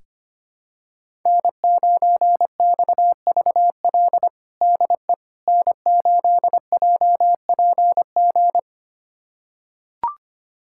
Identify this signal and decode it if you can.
Morse code, heard as a keyed tone, about 25 wpm, 700 Hz. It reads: N9XVL DE N8JPG
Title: USA Callsign DE Callsign - 5 Characters - 25wpm